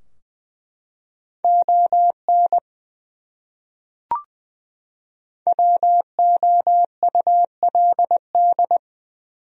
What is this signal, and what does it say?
Morse code, heard as a keyed tone, about 21 wpm, 700 Hz. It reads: ON WOULD
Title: Top 100 Words 20wpm